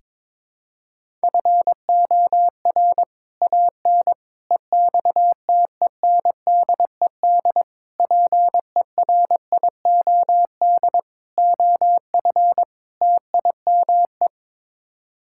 Morse code, heard as a keyed tone, about 22 wpm, 700 Hz. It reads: FOR AN EXTENDED PERIOD OF TIME
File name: practice / 22wpm / Sets of 6 Words - 22wpm